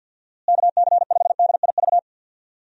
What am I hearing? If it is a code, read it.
Morse code, heard as a keyed tone, about 50 wpm, 700 Hz. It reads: KC5DIV
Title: USA Call Signs - 6 Characters - Repeated 2x - 50wpm